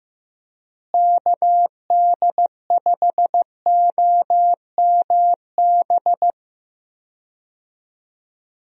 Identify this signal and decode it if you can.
Morse code, heard as a keyed tone, about 15 wpm, 700 Hz. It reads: KD5OMB
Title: USA Call Signs - 6 Characters - Repeated 3x - 15wpm